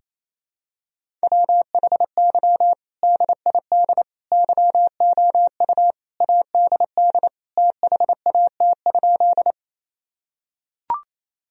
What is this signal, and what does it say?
Morse code, heard as a keyed tone, about 28 wpm, 700 Hz. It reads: WHY DID YOU ADD THAT?
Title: Sentences from Top 1000 Words - Review Part 1 - 28wpm